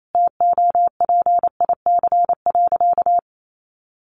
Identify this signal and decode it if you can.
Morse code, heard as a keyed tone, about 28 wpm, 700 Hz. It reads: TOPIC.
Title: Sentences from Top 1200 Words 28wpm